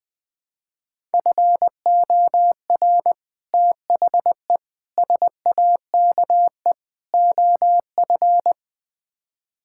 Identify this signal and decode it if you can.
Morse code, heard as a keyed tone, about 20 wpm, 700 Hz. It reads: FOR THE SAKE OF
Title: Build A Phrase - 4-Word Phrases - 20wpm